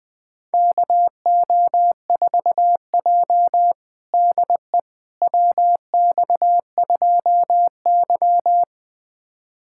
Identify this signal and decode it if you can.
Morse code, heard as a keyed tone, about 20 wpm, 700 Hz. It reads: KO4J DE WX2Y
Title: USA Callsign DE Callsign - 4 Characters - 20wpm